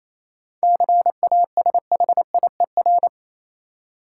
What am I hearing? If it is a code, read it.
Morse code, heard as a keyed tone, about 28 wpm, 700 Hz. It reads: CASHIER
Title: Practice QSO - Exchange 3 - More Info - 28wpm